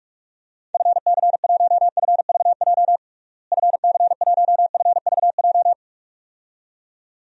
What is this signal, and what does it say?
Morse code, heard as a keyed tone, about 43 wpm, 700 Hz. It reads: FC1FVJ FC1FVJ
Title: DXCC-100 Call Signs - 6 Characters - Repeated 2x - 45wpm